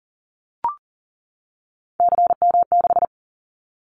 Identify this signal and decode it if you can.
Morse code, heard as a keyed tone, about 38 wpm, 700 Hz. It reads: CM6
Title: Callsign Prefixes for Top 100 DXCC - 40wpm